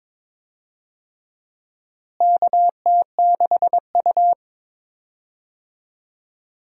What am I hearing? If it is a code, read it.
Morse code, heard as a keyed tone, about 22 wpm, 700 Hz. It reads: KT6U
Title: USA Call Signs - 4 Characters - Repeated 2x - 22wpm